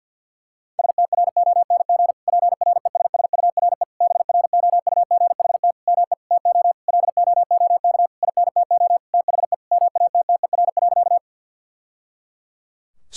Morse code, heard as a keyed tone, about 46 wpm, 700 Hz. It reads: STRONG PRESSURE BROUGHT ME TO LOOK INTO THE MATTER.
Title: Sentences from Top 1000 Words - Review Part 6 - 50wpm